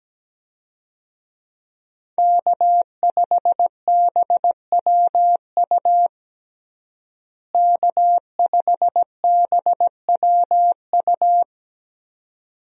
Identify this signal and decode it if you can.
Morse code, heard as a keyed tone, about 17 wpm, 700 Hz. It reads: K5BWU K5BWU
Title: USA Call Signs - 5 Characters - Repeated 2x - 17wpm